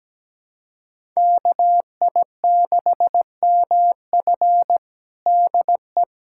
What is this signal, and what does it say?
Morse code, heard as a keyed tone, about 17 wpm, 700 Hz. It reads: KI6MF DE
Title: USA Callsign DE Callsign - 5 Characters - 17wpm